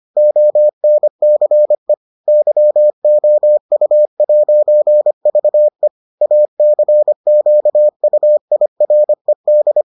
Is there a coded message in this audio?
Yes: Morse code from 0.2 to 9.8 s, heard as a keyed tone, about 25 wpm, 600 Hz. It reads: ONCE YOU'VE ACQUIRED